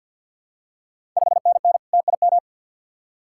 Morse code, heard as a keyed tone, about 53 wpm, 690 Hz. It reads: 5NN TIM
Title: Ontario QSO Party Exchange 50wpm